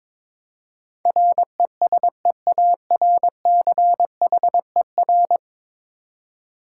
Morse code, heard as a keyed tone, about 22 wpm, 700 Hz. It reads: RESEARCHER
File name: practice / 22wpm / 10 Letter Words - First 3 Letters Given - 22wpm